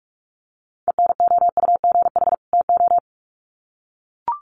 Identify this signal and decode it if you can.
Morse code, heard as a keyed tone, about 44 wpm, 700 Hz. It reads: ENOUGH TO
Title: Sets of 2 Words 45wpm